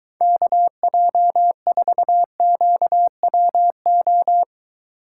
Morse code, heard as a keyed tone, about 23 wpm, 700 Hz. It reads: KJ4QWO